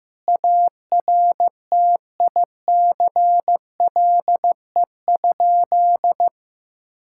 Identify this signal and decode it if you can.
Morse code, heard as a keyed tone, about 15 wpm, 700 Hz. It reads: ARTICLE?
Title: Sentences from Top 1000 Words - Review Part 5 - 15wpm